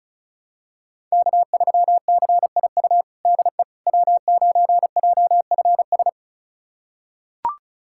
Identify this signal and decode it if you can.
Morse code, heard as a keyed tone, about 34 wpm, 700 Hz. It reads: K3CIU DE W9JFS
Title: USA Callsign DE Callsign - 5 Characters - 35wpm